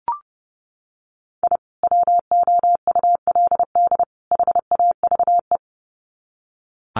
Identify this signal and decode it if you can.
Morse code, heard as a keyed tone, about 30 wpm, 700 Hz. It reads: I WOULD HAVE
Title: Sets of 3 Words 30wpm